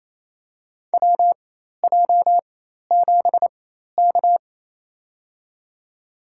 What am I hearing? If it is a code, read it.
Morse code, heard as a keyed tone, about 28 wpm, spaced as for 13 wpm, 700 Hz. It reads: WJ7K
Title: USA Call Signs - 4 Characters - 4x Character Spacing - 28wpm